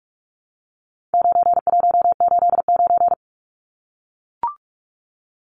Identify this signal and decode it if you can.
Morse code, heard as a keyed tone, about 44 wpm, 700 Hz. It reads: 9189
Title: Numbers - 4-Digit 45wpm